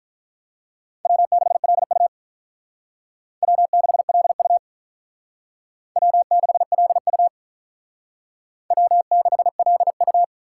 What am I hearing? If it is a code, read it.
Morse code, heard as a keyed tone, about 45 wpm, 700 Hz. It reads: W6LU W6LU W6LU W6LU